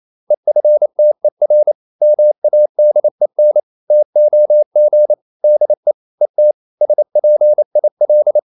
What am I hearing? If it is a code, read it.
Morse code, heard as a keyed tone, about 28 wpm, 600 Hz. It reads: EFTER MADEN TOG DE ET SPIL
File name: QRQ 28wpm